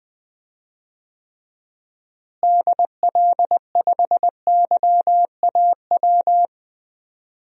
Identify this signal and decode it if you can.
Morse code, heard as a keyed tone, about 20 wpm, 700 Hz. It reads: DL5YAW
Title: DXCC-100 Call Signs - 6 Characters - Repeated 3x - 20wpm